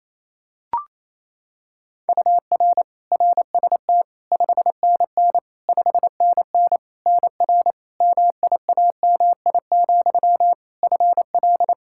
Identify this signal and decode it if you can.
Morse code, heard as a keyed tone, about 28 wpm, 700 Hz. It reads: UR RST 5NN 5NN NR MIAMI, FL
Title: K1USN SST QSO Practice 28wpm